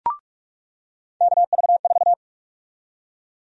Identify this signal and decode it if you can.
Morse code, heard as a keyed tone, about 46 wpm, 700 Hz. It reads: KV4
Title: USA Call Sign Prefixes - 45wpm